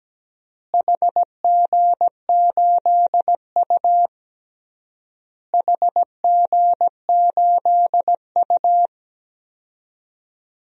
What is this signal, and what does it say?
Morse code, heard as a keyed tone, about 17 wpm, 700 Hz. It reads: HG8U HG8U
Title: DXCC-100 Call Signs - 4 Characters - Repeated 3x - 17wpm